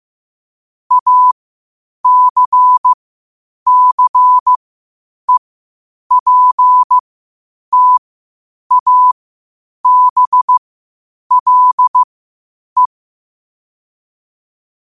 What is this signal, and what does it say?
Morse code, heard as a keyed tone, about 15 wpm, spaced as for 8 wpm, 1000 Hz. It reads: ACCEPTABLE
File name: qotdfiles / 8wpm